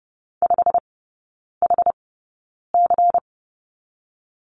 Morse code, heard as a keyed tone, about 30 wpm, spaced as for 9 wpm, 700 Hz. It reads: 5HC
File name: Lesson 16e - Letter c - Three Characters 30wpm